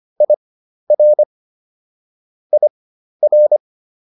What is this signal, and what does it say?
Morse code, heard as a keyed tone, about 25 wpm, spaced as for 12 wpm, 600 Hz. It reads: IR IR